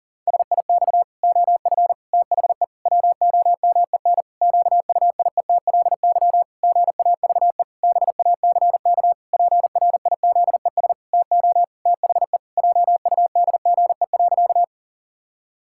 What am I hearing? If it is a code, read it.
Morse code, heard as a keyed tone, about 40 wpm, 700 Hz. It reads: SIX OF THE WOMEN QUIETLY GAVE BACK PRIZES TO THE JUDGE.
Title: Pangram Challenge v2 - 40wpm